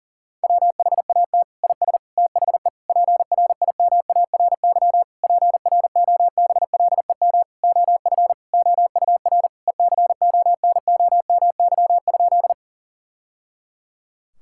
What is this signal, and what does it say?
Morse code, heard as a keyed tone, about 40 wpm, 700 Hz. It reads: WHAT IS THE PRIMARY PROBLEM OF OUR ECONOMY?
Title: Sentences from Top 1000 Words - Review Part 5 - 40wpm